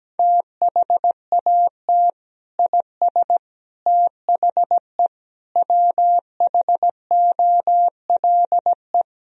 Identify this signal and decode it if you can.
Morse code, heard as a keyed tone, about 17 wpm, 700 Hz. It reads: THAT IS THE WHOLE
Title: Sentences from Top 1000 Words - Review Part 3 - 17wpm